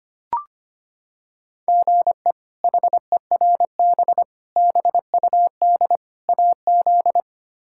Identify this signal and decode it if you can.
Morse code, heard as a keyed tone, about 25 wpm, 700 Hz. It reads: GE HERB BUD AZ
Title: K1USN SST QSO Practice - 25wpm